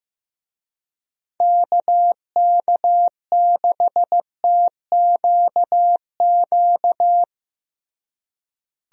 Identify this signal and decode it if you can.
Morse code, heard as a keyed tone, about 15 wpm, 700 Hz. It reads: KK6TQQ
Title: USA Call Signs - 6 Characters - Repeated 3x - 15wpm